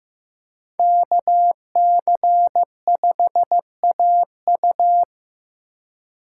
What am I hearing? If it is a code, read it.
Morse code, heard as a keyed tone, about 15 wpm, 700 Hz. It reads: KC5AU
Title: USA Call Signs - 5 Characters - 3x Character Spacing - 15wpm